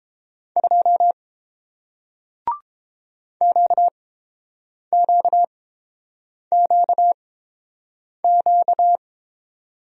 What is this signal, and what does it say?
Morse code, heard as a keyed tone, about 30 wpm, spaced as for 7 wpm, 700 Hz. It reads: 2 QQQQ